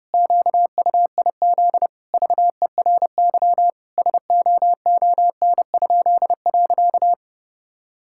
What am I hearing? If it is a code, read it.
Morse code, heard as a keyed tone, about 30 wpm, 700 Hz. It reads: QUIZ VERY SOON?.
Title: Pangram Challenge v2 - 30wpm